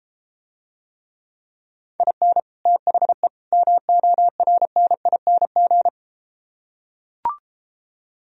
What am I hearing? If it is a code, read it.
Morse code, heard as a keyed tone, about 33 wpm, 700 Hz. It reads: IN THE MORNING